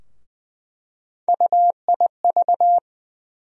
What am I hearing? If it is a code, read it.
Morse code, heard as a keyed tone, about 20 wpm, 700 Hz. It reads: UIV
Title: USA Call Sign Postfixes - 3 Letters 20wpm